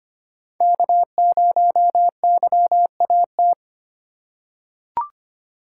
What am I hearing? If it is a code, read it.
Morse code, heard as a keyed tone, about 25 wpm, 700 Hz. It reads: K0YAT
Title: USA Call Signs - 3x Character Spacing - 25wpm